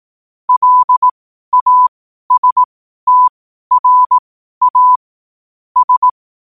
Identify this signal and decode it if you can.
Morse code, heard as a keyed tone, about 18 wpm, spaced as for 12 wpm, 1000 Hz. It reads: LASTRA S
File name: qotdfiles / 12wpm